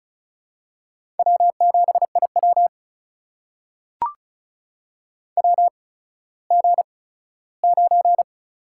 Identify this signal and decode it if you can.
Morse code, heard as a keyed tone, about 36 wpm, spaced as for 17 wpm, 700 Hz. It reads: W7IW WG9
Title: USA Call Signs - 4 Characters - 8x Character Spacing - 35wpm